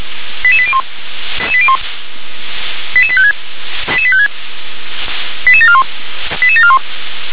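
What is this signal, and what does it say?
Five-tone selective call (ZVEI), heard as quick runs of five tones, tones of about 70 ms. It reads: 88801 88856 88856 88741 88741
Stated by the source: ZVEI five-tone selective call received by SDR.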